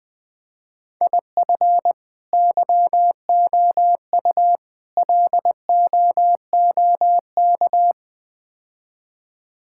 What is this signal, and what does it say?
Morse code, heard as a keyed tone, about 20 wpm, 700 Hz. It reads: IF YOU LOOK
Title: Build A Phrase - 4-Word Phrases - 20wpm